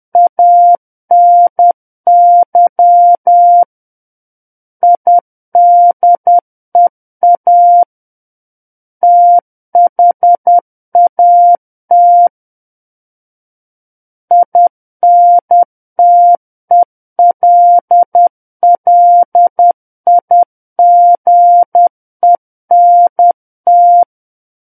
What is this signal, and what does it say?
Morse code, heard as a keyed tone, about 10 wpm, 700 Hz. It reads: ANY IDEA THAT INTELLIGENT
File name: Book One / 01-Chapter 1 - 10 WPM